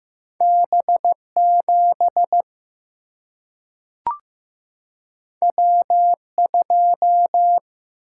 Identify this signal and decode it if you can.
Morse code, heard as a keyed tone, about 15 wpm, 700 Hz. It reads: B 7 W 2
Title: Single-Letter Single-Number 15wpm